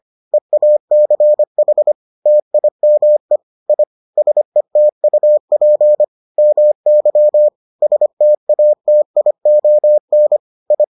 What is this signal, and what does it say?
Morse code, heard as a keyed tone, about 25 wpm, 600 Hz. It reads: EACH TIME I SETUP MY STATION I